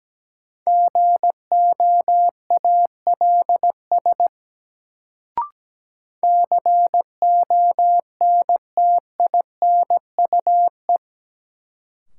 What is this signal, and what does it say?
Morse code, heard as a keyed tone, about 17 wpm, 700 Hz. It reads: GOALS CONTINUE
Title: Sentences from Top 1000 Words - Review Part 6 - New Word Drill - 17wpm